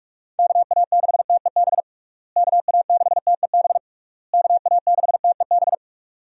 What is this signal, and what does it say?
Morse code, heard as a keyed tone, about 46 wpm, 700 Hz. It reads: KA6TEB KA6TEB KA6TEB
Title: USA Call Signs - 6 Characters - Repeated 3x - 45wpm